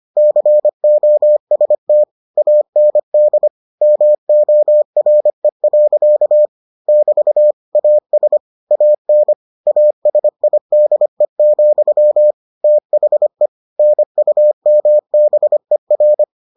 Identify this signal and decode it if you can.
Morse code, heard as a keyed tone, about 25 wpm, 600 Hz. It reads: COST AND MORE. <BT> AS AN ASIDE, THE NUMBER